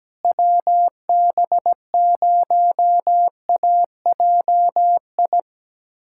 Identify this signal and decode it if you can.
Morse code, heard as a keyed tone, about 17 wpm, 700 Hz. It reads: WB0AJI
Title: USA Callsign DE Callsign - 6 Characters - 17wpm